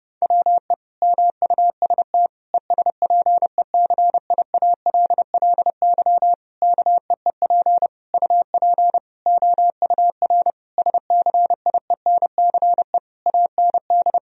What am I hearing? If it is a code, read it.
Morse code, heard as a keyed tone, about 30 wpm, 700 Hz. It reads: WE MUST ESPECIALLY KEEP UP OUR SCIENCE AND
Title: Sentences from Top 1000 Words - Review Part 4 - 30wpm